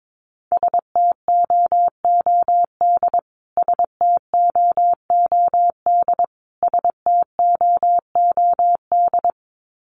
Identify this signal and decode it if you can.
Morse code, heard as a keyed tone, about 22 wpm, 700 Hz. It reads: STOOD STOOD STOOD
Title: Top 500 Words Repeated 22wpm